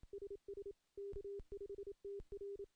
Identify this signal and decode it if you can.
Morse code, heard as a keyed tone, about 27 wpm, 400 Hz. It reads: SS K5TR
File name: contests / CW-SS-k5tr- -1300